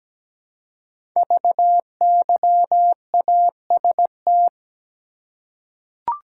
Morse code, heard as a keyed tone, about 17 wpm, 700 Hz. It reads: VYAST
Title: Code Groups - 5 Letters - 17wpm